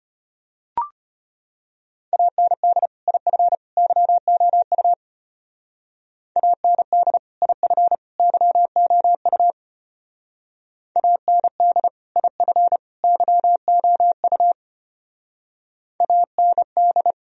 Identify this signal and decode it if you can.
Morse code, heard as a keyed tone, about 31 wpm, 700 Hz. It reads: AND IF YOU AND IF YOU AND IF YOU AND
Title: Speed-Racing - Sets of 3 Words 25wpm